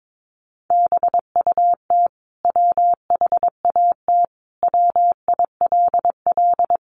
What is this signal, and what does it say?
Morse code, heard as a keyed tone, about 22 wpm, 700 Hz. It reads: BUT WHAT WILL
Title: Sentences from Top 100 Words 22wpm